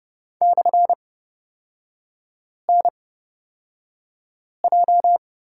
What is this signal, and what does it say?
Morse code, heard as a keyed tone, about 30 wpm, spaced as for 5 wpm, 700 Hz. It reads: /NJ
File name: Lesson 36e - BT - Three Characters 30wpm